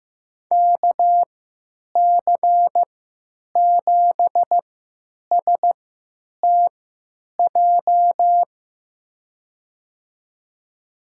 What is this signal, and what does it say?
Morse code, heard as a keyed tone, about 15 wpm, spaced as for 9 wpm, 700 Hz. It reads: KC7STJ
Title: USA Call Signs - 6 Characters - 3x Character Spacing - 15wpm